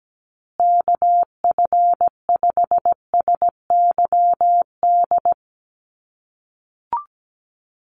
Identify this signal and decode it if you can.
Morse code, heard as a keyed tone, about 17 wpm, 700 Hz. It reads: KF5SYD
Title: USA Call Signs Repeated 2x 17wpm